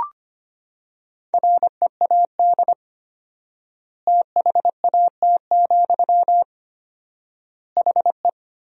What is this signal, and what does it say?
Morse code, heard as a keyed tone, about 25 wpm, 700 Hz. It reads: READ THAT, HE
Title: 4x Word-Spacing - Sentences from Top 200 Words - 25wpm